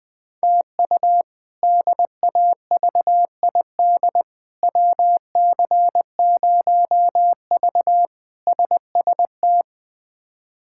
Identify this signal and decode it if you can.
Morse code, heard as a keyed tone, about 21 wpm, 700 Hz. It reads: TU DAVID WC0V SST
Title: K1USN SST QSO Practice - 20wpm